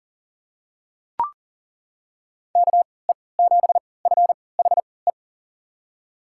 Morse code, heard as a keyed tone, about 39 wpm, spaced as for 22 wpm, 700 Hz. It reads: KE7FHE
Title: USA Call Signs - 3x Character Spacing - 40wpm